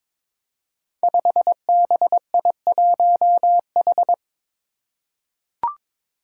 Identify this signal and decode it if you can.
Morse code, heard as a keyed tone, about 22 wpm, 700 Hz. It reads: 5BI1H
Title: Five Challenging Letter-Number Combinations 22wpm